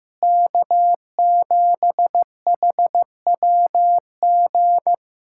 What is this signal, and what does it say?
Morse code, heard as a keyed tone, about 15 wpm, 700 Hz. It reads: K7HWG
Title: USA Callsign DE Callsign - 5 Characters - 15wpm